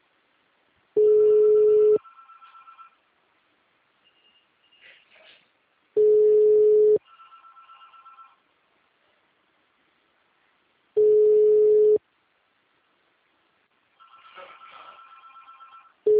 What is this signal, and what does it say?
Ringback tone (Europe, 425 Hz).